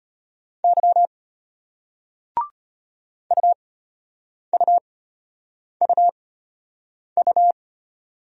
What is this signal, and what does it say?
Morse code, heard as a keyed tone, about 33 wpm, spaced as for 8 wpm, 700 Hz. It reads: Y UUUU